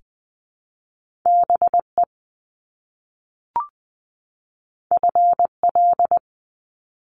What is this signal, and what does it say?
Morse code, heard as a keyed tone, about 20 wpm, 700 Hz. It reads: BE FL
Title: Top Two Letter Combinations in Words 20wpm